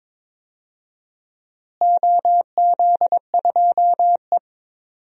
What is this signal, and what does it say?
Morse code, heard as a keyed tone, about 21 wpm, 700 Hz. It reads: OZ2E
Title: DXCC-100 Call Signs - 4 Characters - Repeated 3x - 22wpm